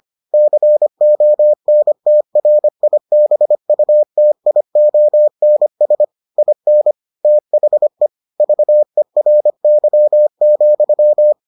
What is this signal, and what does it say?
Morse code, heard as a keyed tone, about 25 wpm, 600 Hz. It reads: CONTRIBUTIONS IN THE VERY,